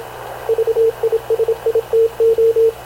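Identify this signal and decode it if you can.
Morse code, heard as a keyed tone, about 26 wpm, 440 Hz, with radio noise. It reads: VISITO
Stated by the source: Havana Radio / CLA sending Spanish press (PX) in Morse, recorded by Radio Officer Paul Dumesnil onboard CNAV QUEST/CZDO.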